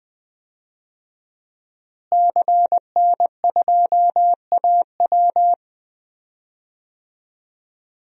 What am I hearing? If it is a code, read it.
Morse code, heard as a keyed tone, about 21 wpm, 700 Hz. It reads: CN2AW
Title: DXCC-100 Call Signs - 5 Characters - Repeated 2x - 20wpm